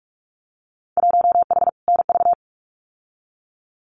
Morse code, heard as a keyed tone, about 46 wpm, 700 Hz. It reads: 1H NV
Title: Winter Field Day Exchanges 45wpm